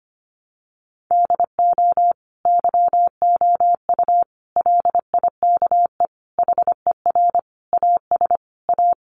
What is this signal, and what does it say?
Morse code, heard as a keyed tone, about 25 wpm, 700 Hz. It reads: DO YOU LIKE HER AS A
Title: Sentences from Top 1100 Words 25wpm